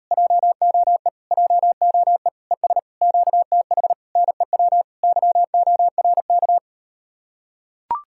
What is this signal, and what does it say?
Morse code, heard as a keyed tone, about 39 wpm, 700 Hz. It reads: JOE JOE ES QTH NEW YORK